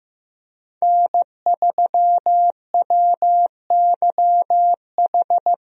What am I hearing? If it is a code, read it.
Morse code, heard as a keyed tone, about 15 wpm, 700 Hz. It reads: N3WYH
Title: USA Call Signs - 5 Characters - Repeated 3x - 15wpm